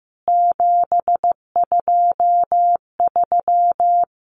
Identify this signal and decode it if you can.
Morse code, heard as a keyed tone, about 15 wpm, 700 Hz. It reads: 723
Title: CWT Exchange 15wpm